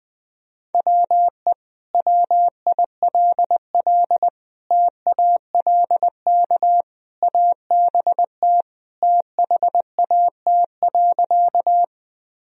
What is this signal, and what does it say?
Morse code, heard as a keyed tone, about 20 wpm, 700 Hz. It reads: WE WILL TALK ABT THAT.
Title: Sentences from Top 400 Words and Abbreviations 20wpm